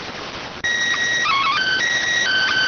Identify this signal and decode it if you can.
SSTV (slow-scan television), PD 290.